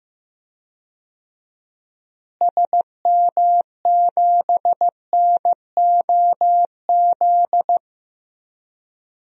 Morse code, heard as a keyed tone, about 15 wpm, 700 Hz. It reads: SM7NOZ
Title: DXCC-100 Call Signs - 6 Characters - Repeated 3x - 15wpm